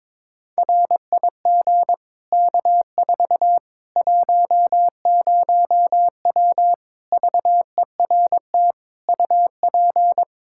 Morse code, heard as a keyed tone, about 22 wpm, 700 Hz. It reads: RIG K4 10W VERT UP
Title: Practice QSO - Exchange 2 - Info - 22wpm